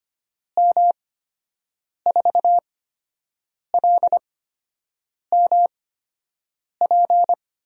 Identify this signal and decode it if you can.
Morse code, heard as a keyed tone, about 25 wpm, spaced as for 7 wpm, 700 Hz. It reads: M4LMP
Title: USA Call Signs - 6 Characters - 8x Character Spacing - 25wpm